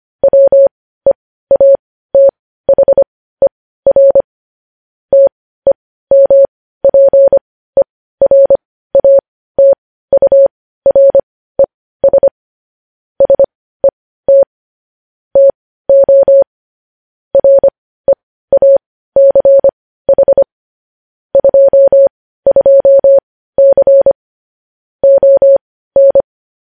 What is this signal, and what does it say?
Morse code, heard as a keyed tone, about 25 wpm, spaced as for 15 wpm, 550 Hz. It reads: WEATHER TEMPERATURES SET TO REACH 22C ON
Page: news headlines in morse code at 15 wpm